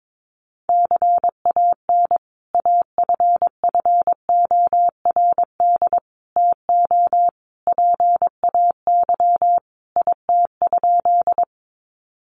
Morse code, heard as a keyed tone, about 22 wpm, 700 Hz. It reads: CAN AFFORD TO PAY IT?
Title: Sentences from Top 1100 Words 22wpm